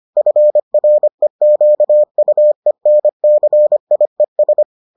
Morse code, heard as a keyed tone, about 25 wpm, 600 Hz. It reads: FREQUENCIES